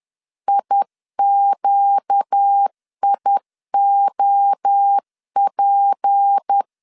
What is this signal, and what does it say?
Morse code, heard as a keyed tone, about 10 wpm, 800 Hz. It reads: IQIOP
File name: CW / DET / 1 6 1500C S C 10wpm 4252